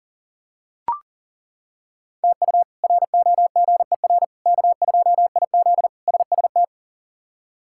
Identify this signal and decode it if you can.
Morse code, heard as a keyed tone, about 40 wpm, 700 Hz. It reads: TU ROGER K2IZ SST
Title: K1USN SST QSO Practice - 40wpm